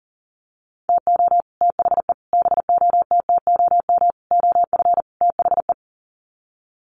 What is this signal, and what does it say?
Morse code, heard as a keyed tone, about 40 wpm, 700 Hz. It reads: TO THE BOTTOM OF THE
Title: Sets of 5 Words 40wpm